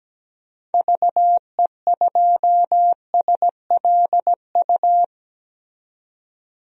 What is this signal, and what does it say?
Morse code, heard as a keyed tone, about 17 wpm, 700 Hz. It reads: VE2SLU
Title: DXCC-100 Call Signs - 6 Characters - 17wpm